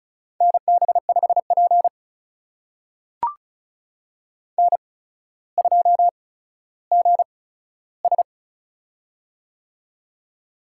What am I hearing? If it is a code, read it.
Morse code, heard as a keyed tone, about 36 wpm, spaced as for 15 wpm, 700 Hz. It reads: NB5P N2GS
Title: USA Call Signs - 4 Characters - 8x Character Spacing - 35wpm